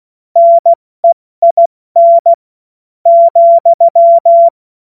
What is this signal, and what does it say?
Morse code, heard as a keyed tone, about 16 wpm, spaced as for 12 wpm, 680 Hz. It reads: NEIN,
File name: Witz-26-August-2025-70-F80